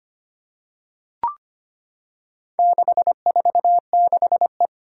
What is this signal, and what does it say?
Morse code, heard as a keyed tone, about 25 wpm, 700 Hz. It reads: 646E
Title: Four Challenging Letter-Number Combinations 25wpm